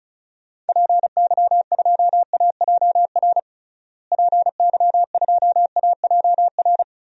Morse code, heard as a keyed tone, about 36 wpm, 700 Hz. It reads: PY2AJR PY2AJR
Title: DXCC-100 Call Signs - 6 Characters - Repeated 3x - 35wpm